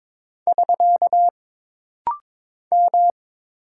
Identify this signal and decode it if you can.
Morse code, heard as a keyed tone, about 22 wpm, 700 Hz. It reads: <SK> M
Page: This rapid-fire timing is geared to push your ICR (Instant Character Recognition) skills.